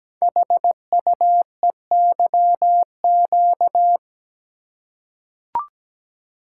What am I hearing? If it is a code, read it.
Morse code, heard as a keyed tone, about 17 wpm, 700 Hz. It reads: HUEYQ
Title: Code Groups - 5 Letters - 17wpm